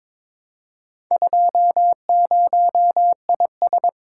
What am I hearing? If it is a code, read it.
Morse code, heard as a keyed tone, about 23 wpm, 700 Hz. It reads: 20IS
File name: practice / 22wpm / Four Letter-Number Combinations - 22wpm